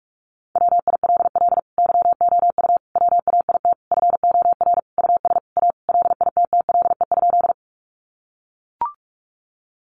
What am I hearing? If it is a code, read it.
Morse code, heard as a keyed tone, about 44 wpm, 700 Hz. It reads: WILL YOU WAIT FOR US A LITTLE?
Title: Sentences from Top 900 Words 45wpm